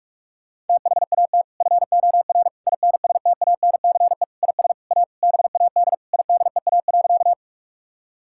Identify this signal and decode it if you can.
Morse code, heard as a keyed tone, about 46 wpm, 700 Hz. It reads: THAT FOR INSTANCE IS A BAD IDEA.
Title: Sentences from Top 1000 Words - Review Part 5 - 45wpm